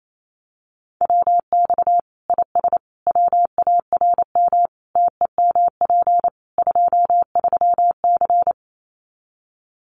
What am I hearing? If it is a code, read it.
Morse code, heard as a keyed tone, about 28 wpm, 700 Hz. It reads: WX IS WARM TEMP 23C
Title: Common QSO Phrases 28wpm